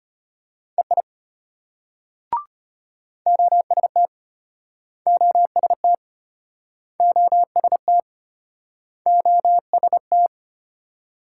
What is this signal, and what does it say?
Morse code, heard as a keyed tone, about 33 wpm, 700 Hz. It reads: EI OST OST OST OST